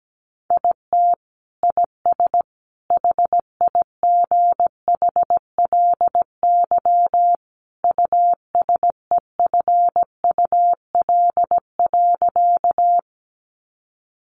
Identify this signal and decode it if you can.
Morse code, heard as a keyed tone, about 17 wpm, 700 Hz. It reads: IT IS HIGHLY USEFUL.
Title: Sentences from Top 1100 Words 17wpm